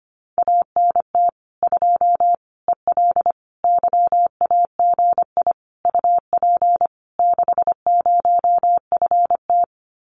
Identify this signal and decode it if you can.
Morse code, heard as a keyed tone, about 25 wpm, 700 Hz. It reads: ANT 2 EL YAGI UP 60FT
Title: Common QSO Phrases Encore 25wpm